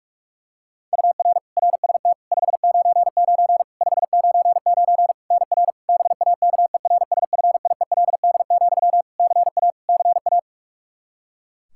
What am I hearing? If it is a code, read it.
Morse code, heard as a keyed tone, about 46 wpm, 700 Hz. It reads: UR RST 599 599 NR BAKERSFIELD, CA CA
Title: Practice QSO - Exchange 1 - Beginning - 45wpm